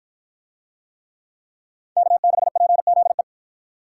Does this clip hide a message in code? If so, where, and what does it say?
Yes: Morse code from 2.0 to 3.2 s, heard as a keyed tone, about 52 wpm, 700 Hz. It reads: K6PZE